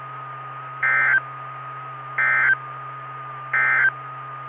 Emergency Alert System (SAME) tones, heard as harsh buzzing bursts. It is heard from 0.7 to 4.0 s.
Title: Saline, AK SKYWARN Repeater System - W5RHS